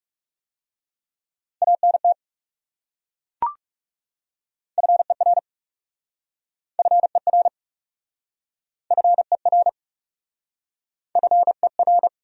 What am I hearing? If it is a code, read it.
Morse code, heard as a keyed tone, about 40 wpm, 700 Hz. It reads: ANT FER FER FER FER